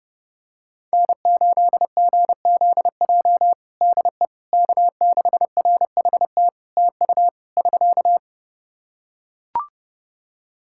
Morse code, heard as a keyed tone, about 28 wpm, 700 Hz. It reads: N8GZJ DE K6RHT TU <SK>
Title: Practice QSO - Exchange 4 - Ending - 30wpm